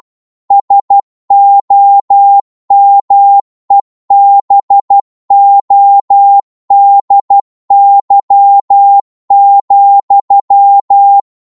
Morse code, heard as a keyed tone, about 12 wpm, 800 Hz. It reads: SOMEBODY,
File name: cwmp3 / psg-1-12wpm